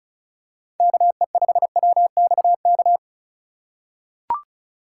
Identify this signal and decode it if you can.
Morse code, heard as a keyed tone, about 34 wpm, 700 Hz. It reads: KE5WXK
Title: USA Call Signs - 6 Characters - 2x Character Spacing - 35wpm